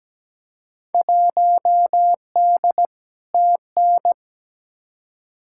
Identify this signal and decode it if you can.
Morse code, heard as a keyed tone, about 17 wpm, 700 Hz. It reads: 1D TN
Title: Field Day Exchanges - 17wpm